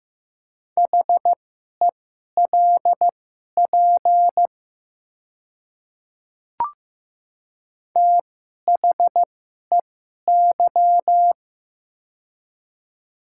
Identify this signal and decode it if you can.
Morse code, heard as a keyed tone, about 15 wpm, spaced as for 11 wpm, 700 Hz. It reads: HELP THEY
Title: 2x Character Spacing - Top 300 Words - 15wpm